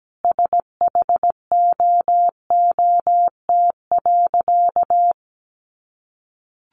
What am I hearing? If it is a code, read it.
Morse code, heard as a keyed tone, about 17 wpm, 700 Hz. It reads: SHOOT.
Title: Sentences from Top 1200 Words 17wpm